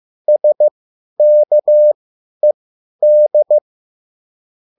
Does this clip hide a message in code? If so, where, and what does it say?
Yes: Morse code from 0.3 to 3.6 s, heard as a keyed tone, about 15 wpm, spaced as for 10 wpm, 600 Hz. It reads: SKED